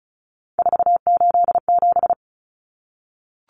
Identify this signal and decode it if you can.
Morse code, heard as a keyed tone, about 34 wpm, 700 Hz. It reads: 487
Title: Common Cut Numbers - 3-Digit 35wpm